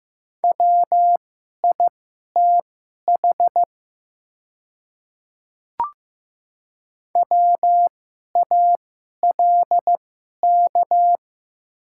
Morse code, heard as a keyed tone, about 15 wpm, spaced as for 11 wpm, 700 Hz. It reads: WITH WALK
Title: 2x Character Spacing - Top 500 Words - 15wpm